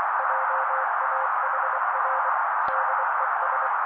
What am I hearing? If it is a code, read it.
Morse code, heard as a keyed tone, about 24 wpm, 530 Hz. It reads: JAHR DES
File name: daily / 23wpm noise